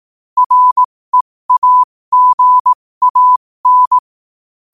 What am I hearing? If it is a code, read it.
Morse code, heard as a keyed tone, about 18 wpm, spaced as for 15 wpm, 1000 Hz. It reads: REAGAN
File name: qotdfiles / 15wpm